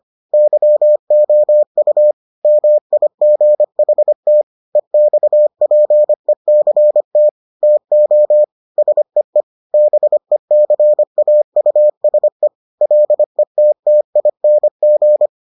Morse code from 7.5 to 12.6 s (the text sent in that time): TO SEE BECAUSE